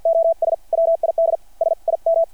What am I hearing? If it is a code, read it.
Morse code, heard as a keyed tone, about 50 wpm, spaced as for 44 wpm, 660 Hz. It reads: OS WID SIM
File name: archived / CRSnet